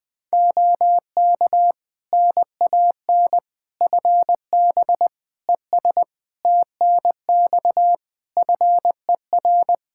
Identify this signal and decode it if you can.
Morse code, heard as a keyed tone, about 20 wpm, 700 Hz. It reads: OK NAN FB ES TNX FER
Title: Practice QSO - Exchange 2 - Info - 20wpm